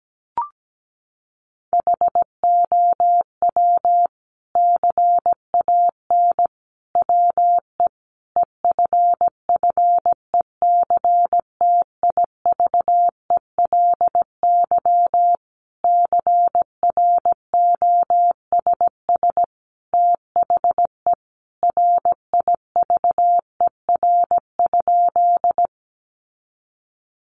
Morse code, heard as a keyed tone, about 17 wpm, 700 Hz. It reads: HOW CAN WE EFFECTIVELY CROSS THE RIVER?
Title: Sentences from Top 1400 Words 17wpm